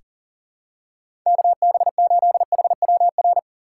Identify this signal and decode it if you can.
Morse code, heard as a keyed tone, about 40 wpm, 700 Hz. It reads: KB8HWR
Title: USA Call Signs - 6 Characters - 3x Character Spacing - 40wpm